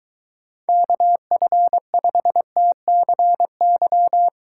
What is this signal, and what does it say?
Morse code, heard as a keyed tone, about 23 wpm, 700 Hz. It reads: KF5TCY